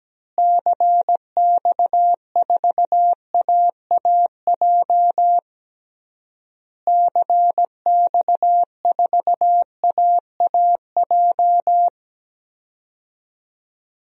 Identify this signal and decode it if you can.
Morse code, heard as a keyed tone, about 17 wpm, 700 Hz. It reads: CX4AAJ CX4AAJ
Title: DXCC-100 Call Signs - 6 Characters - Repeated 3x - 17wpm